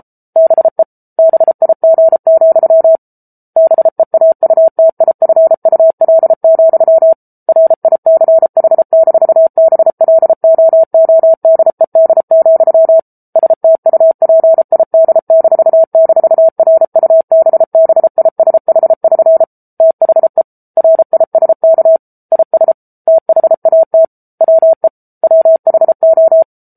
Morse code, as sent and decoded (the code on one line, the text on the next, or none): -... / -... .. --. --..-- / -... .- ..- - .. ..-. ..- .-.. --..-- / .-. .. -.-. .... -....- -... .-.. --- --- -.. -.. --..-- / ... - ..- .--. .. -.. -....- -....- .-. ..- -... -... .. ... .... ...-. / - .... / .-. .. ... -.- / .. ... / - .... .- - / .-- . / .-- .... ---
BE BIG, BEAUTIFUL, RICH-BLOODED, STUPID--RUBBISH<SN> THE RISK IS THAT WE WHO